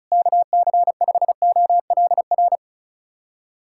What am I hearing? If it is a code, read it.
Morse code, heard as a keyed tone, about 36 wpm, 690 Hz. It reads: KC5OLR
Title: USA Callsign DE Callsign - 6 Characters - 35wpm